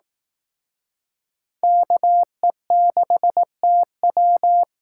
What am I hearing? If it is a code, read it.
Morse code, heard as a keyed tone, about 18 wpm, 700 Hz. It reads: KE6TW
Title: Speed Racing - USA 2x2 Call Signs 15wpm